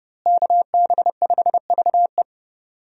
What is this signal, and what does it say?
Morse code, heard as a keyed tone, about 31 wpm, 700 Hz. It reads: KB5VE
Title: USA Call Signs - 5 Characters - Repeated 3x - 30wpm